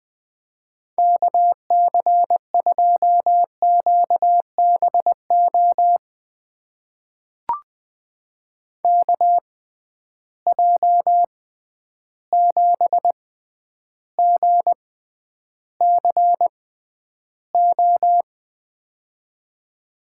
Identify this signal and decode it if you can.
Morse code, heard as a keyed tone, about 20 wpm, spaced as for 10 wpm, 700 Hz. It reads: KC2QBO KJ7GCO
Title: USA Call Signs - 6 Characters - 6x Character Spacing - 20wpm